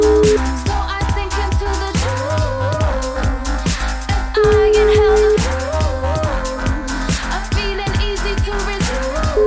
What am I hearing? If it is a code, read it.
Ringback tone (Europe, 425 Hz), with noise.